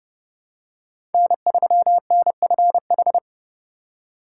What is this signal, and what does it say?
Morse code, heard as a keyed tone, about 30 wpm, 700 Hz. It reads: N3NFH
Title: USA Call Signs - 5 Characters - 4x Character Spacing - 30wpm